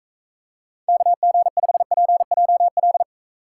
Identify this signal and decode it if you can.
Morse code, heard as a keyed tone, about 42 wpm, 700 Hz. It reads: KG5PJL